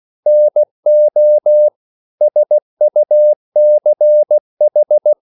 Morse code, heard as a keyed tone, about 16 wpm, 600 Hz. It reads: NO SUCH
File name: CW / Practice / EnglishSayings16